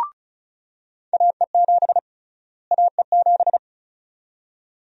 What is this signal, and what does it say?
Morse code, heard as a keyed tone, about 36 wpm, 700 Hz. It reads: AE7 AE7
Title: USA Call Sign Prefixes - Repeated 2x - 35wpm